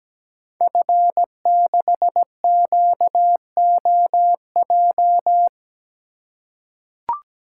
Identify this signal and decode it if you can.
Morse code, heard as a keyed tone, about 17 wpm, 700 Hz. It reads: F6QOJ
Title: Code Groups - 5 Letter:Numbers - 17wpm